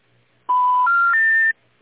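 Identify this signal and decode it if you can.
Special information tone.
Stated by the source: Telephone call progress tone recorded in Virginia, modern North American telephone system.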